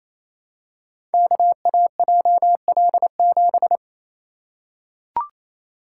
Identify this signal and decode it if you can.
Morse code, heard as a keyed tone, about 28 wpm, 700 Hz. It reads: KAJL7
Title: Code Groups - 5 Letter:Numbers - 28wpm